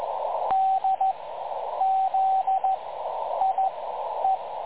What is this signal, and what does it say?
Morse code, heard as a keyed tone, about 14 wpm, spaced as for 9 wpm, 750 Hz, with radio noise. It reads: DZIE
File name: dlazaawansowanych / nagraniazpasma / websdr recording 3565.0kHz